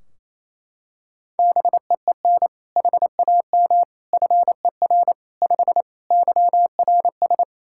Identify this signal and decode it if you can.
Morse code, heard as a keyed tone, about 28 wpm, 700 Hz. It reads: BEEN HAM FER 5 YRS
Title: K1USN SST QSO Practice 28wpm